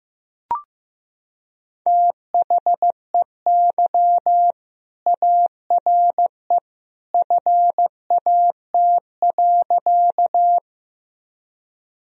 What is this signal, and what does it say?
Morse code, heard as a keyed tone, about 16 wpm, 700 Hz. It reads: THEY ARE FAT.
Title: Sentences from Top 1000 Words - Review Part 4 - 15wpm